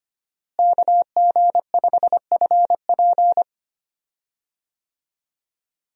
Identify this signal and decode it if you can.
Morse code, heard as a keyed tone, about 25 wpm, 700 Hz. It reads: KG5FP
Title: USA Call Signs - 5 Characters - Repeated 3x - 25wpm